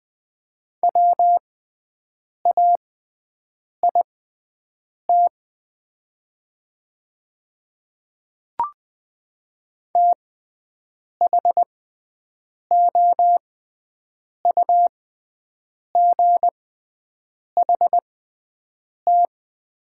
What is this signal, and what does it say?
Morse code, heard as a keyed tone, about 20 wpm, spaced as for 7 wpm, 700 Hz. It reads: WAIT THOUGHT
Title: 6x Character Spacing - Top 500 Words - 20wpm